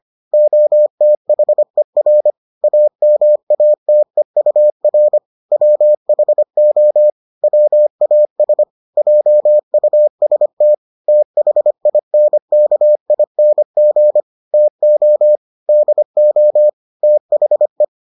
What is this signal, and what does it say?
Morse code, heard as a keyed tone, about 25 wpm, 600 Hz. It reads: OTHER AMATEUR WHO WAS JUST THINKING TO DO THE